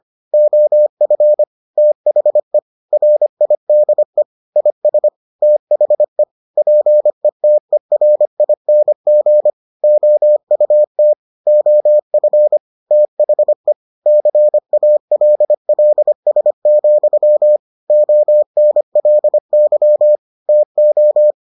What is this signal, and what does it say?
Morse code, heard as a keyed tone, about 25 wpm, 600 Hz. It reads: OF THE RIDE IS THE PETERING OUT OF THE CALLS, ONLY TO